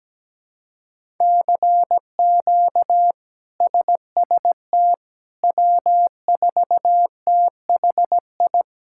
Morse code, heard as a keyed tone, about 17 wpm, 700 Hz. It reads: CQ SST W4THI
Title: K1USN SST QSO Practice - 17wpm